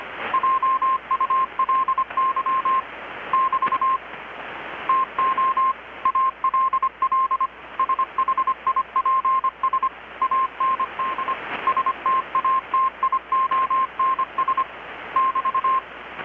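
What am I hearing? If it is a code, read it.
Morse code, heard as a keyed tone, about 25 wpm, 1060 Hz, with radio noise. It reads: JULY <BT> TO ALL SHIPS AND STATIONS <BT>
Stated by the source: Morse from KPH's Night of Nights XX maritime radiotelegraphy event, received in Chiapas, Mexico about 3,800 km away on a Kenwood R-600 with a grounded 10 m long-wire antenna.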